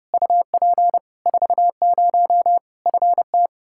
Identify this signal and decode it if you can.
Morse code, heard as a keyed tone, about 32 wpm, 700 Hz. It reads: UP 40 FT
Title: K1USN SST QSO Practice 30wpm